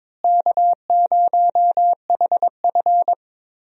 Morse code, heard as a keyed tone, about 22 wpm, 700 Hz. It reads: K0HF
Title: USA Call Signs - 4 Characters - Repeated 2x - 22wpm